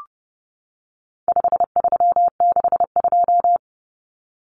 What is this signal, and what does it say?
Morse code, heard as a keyed tone, about 30 wpm, 700 Hz. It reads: 5362
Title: Common Cut Numbers - 4-Digit 30wpm